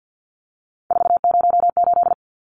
Morse code, heard as a keyed tone, about 50 wpm, 700 Hz. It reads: 408
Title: Numbers - 3-Digit 50wpm